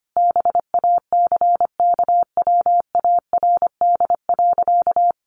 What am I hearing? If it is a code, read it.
Morse code, heard as a keyed tone, about 25 wpm, 700 Hz. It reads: BACKWARD.
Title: A Sound Of Thunder 25wpm